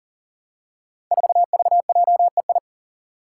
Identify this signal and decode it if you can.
Morse code, heard as a keyed tone, about 38 wpm, 700 Hz. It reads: 4VJEI
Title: Five Challenging Letter-Number Combinations 40wpm